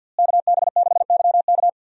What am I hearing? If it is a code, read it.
Morse code, heard as a keyed tone, about 48 wpm, 700 Hz. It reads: KB6YK
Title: USA Call Signs - 5 Characters - Repeated 3x - 50wpm